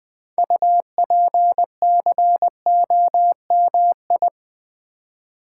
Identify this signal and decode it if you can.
Morse code, heard as a keyed tone, about 20 wpm, 700 Hz. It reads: UPCOMI
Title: Build A Word - 2 Letters At A Time - 8-Letter Words - 20wpm